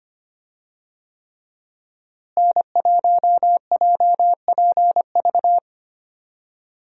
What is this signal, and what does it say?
Morse code, heard as a keyed tone, about 25 wpm, 700 Hz. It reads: N1JPV
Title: USA Call Signs - 5 Characters - Repeated 3x - 25wpm